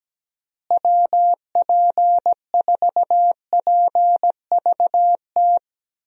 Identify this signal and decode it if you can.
Morse code, heard as a keyed tone, about 17 wpm, 700 Hz. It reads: WP4PVT
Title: USA Call Signs - 6 Characters - Repeated 2x - 17wpm